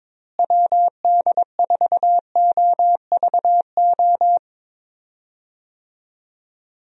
Morse code, heard as a keyed tone, about 22 wpm, 700 Hz. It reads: WD4OVO
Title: USA Call Signs - 6 Characters - Repeated 2x - 22wpm